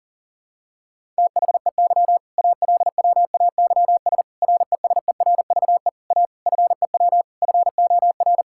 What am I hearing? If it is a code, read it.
Morse code, heard as a keyed tone, about 37 wpm, 700 Hz. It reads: THEY ALWAYS RESERVE A FEW FOR
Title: Sentences from Top 1000 Words - Review Part 6 - 40wpm